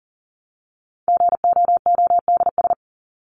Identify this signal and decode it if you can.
Morse code, heard as a keyed tone, about 40 wpm, 700 Hz. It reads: GOODS
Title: ES:S Suffix - Top 100 Words with Base Words 40wpm